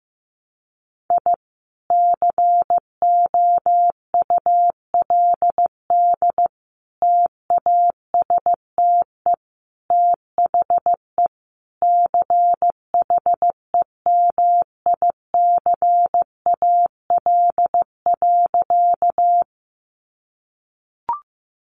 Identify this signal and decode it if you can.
Morse code, heard as a keyed tone, about 15 wpm, 700 Hz. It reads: I COULD TASTE THE CHEMICAL.
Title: Sentences from Top 1200 Words 15wpm